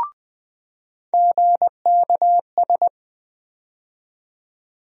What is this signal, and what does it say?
Morse code, heard as a keyed tone, about 20 wpm, 700 Hz. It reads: GKS
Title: USA Call Sign Postfixes - 3 Letters 20wpm